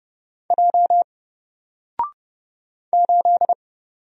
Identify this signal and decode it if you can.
Morse code, heard as a keyed tone, about 30 wpm, 700 Hz. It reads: J 8
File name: Single-Letter-Number-30wpm